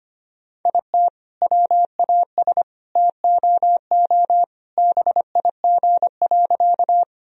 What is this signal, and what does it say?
Morse code, heard as a keyed tone, about 25 wpm, 700 Hz. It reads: IT WAS TOO BIG.
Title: Sentences from Top 1000 Words - Review Part 1 - 25wpm